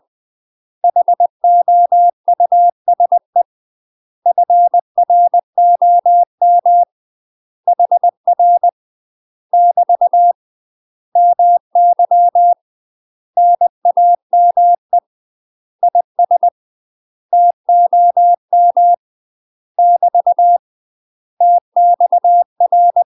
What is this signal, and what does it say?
Morse code, heard as a keyed tone, about 20 wpm, 700 Hz. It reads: HOUSE FROM HR <BT> MY NAME IS TOM <BT> TXR
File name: UK MASTER QSOs TRAINING / 20WPM / UK MASTER QSOs 0043